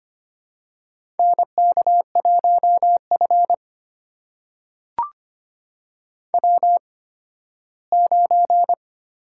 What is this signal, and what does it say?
Morse code, heard as a keyed tone, about 25 wpm, spaced as for 15 wpm, 700 Hz. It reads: NK1F W9
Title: USA Call Signs - 4 Characters - 8x Character Spacing - 25wpm